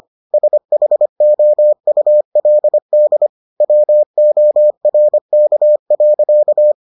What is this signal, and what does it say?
Morse code, heard as a keyed tone, about 25 wpm, 600 Hz. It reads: SHOULD WORK.